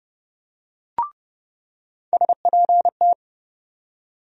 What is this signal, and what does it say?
Morse code, heard as a keyed tone, about 30 wpm, 700 Hz. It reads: SPT
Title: USA Call Sign Postfixes - 3 Letters 30wpm